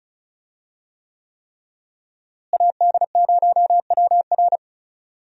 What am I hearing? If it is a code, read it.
Morse code, heard as a keyed tone, about 34 wpm, 700 Hz. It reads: AD0WR
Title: USA Call Signs - 5 Characters - Repeated 3x - 35wpm